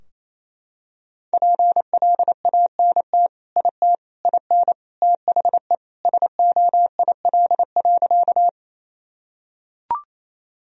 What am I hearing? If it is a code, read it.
Morse code, heard as a keyed tone, about 28 wpm, 700 Hz. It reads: PLANT IT IN THE SOIL.
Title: Sentences from Top 1000 Words - Review Part 4 - 28wpm